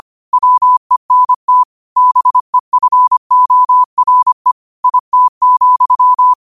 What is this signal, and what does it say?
Morse code, heard as a keyed tone, about 25 wpm, 1000 Hz. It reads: WENT BEFORE IT,